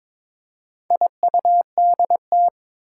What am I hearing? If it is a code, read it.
Morse code, heard as a keyed tone, about 22 wpm, 700 Hz. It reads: IUDT
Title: Four Challenging Letter-Number Combinations 22wpm